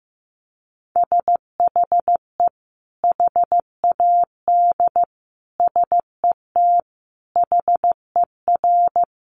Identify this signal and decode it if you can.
Morse code, heard as a keyed tone, about 15 wpm, 700 Hz. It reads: SHE HAD SET HER
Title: Sentences from Top 1200 Words 15wpm